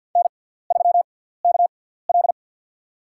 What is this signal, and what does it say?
Morse code, heard as a keyed tone, about 47 wpm, spaced as for 17 wpm, 700 Hz. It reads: N3KL
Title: USA Call Signs - 6x Character Spacing - 50wpm